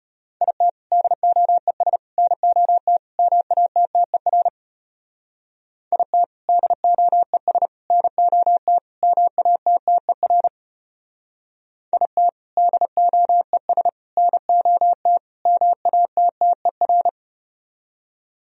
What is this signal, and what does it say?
Morse code, heard as a keyed tone, about 33 wpm, 700 Hz. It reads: IT DOES NOT MATTER IT DOES NOT MATTER IT DOES NOT MATTER